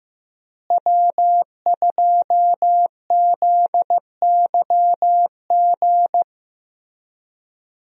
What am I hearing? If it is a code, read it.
Morse code, heard as a keyed tone, about 15 wpm, 700 Hz. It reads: W2ZYG
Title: USA Call Signs - 5 Characters - Repeated 2x - 15wpm